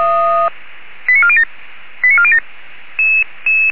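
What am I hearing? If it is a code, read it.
Five-tone selective call (ZVEI), heard as quick runs of five tones, tones of about 70 ms. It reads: 89407 89407
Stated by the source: ZVEI five-tone selective call received with an SDR.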